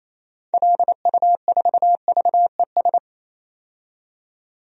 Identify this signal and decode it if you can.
Morse code, heard as a keyed tone, about 28 wpm, 700 Hz. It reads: LU4VES
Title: DXCC-100 Call Signs - 6 Characters - 28wpm